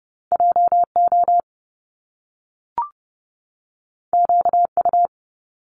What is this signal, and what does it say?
Morse code, heard as a keyed tone, about 30 wpm, 700 Hz. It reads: JO QU
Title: Top Two Letter Combinations in Words 30wpm